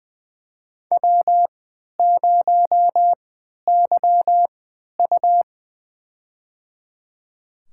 Morse code, heard as a keyed tone, about 20 wpm, spaced as for 11 wpm, 700 Hz. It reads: W0YU
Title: USA Call Signs - 4 Characters - 3x Character Spacing - 20wpm